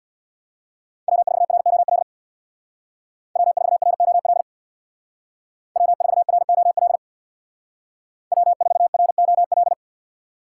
Morse code, heard as a keyed tone, about 64 wpm, 700 Hz. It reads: W4ROL W4ROL W4ROL W4ROL